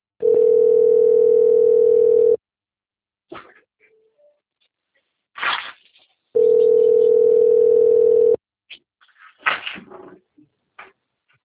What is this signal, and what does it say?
Ringback tone (North America), 440 + 480 Hz.
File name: rg-1600-unknown- - - .48962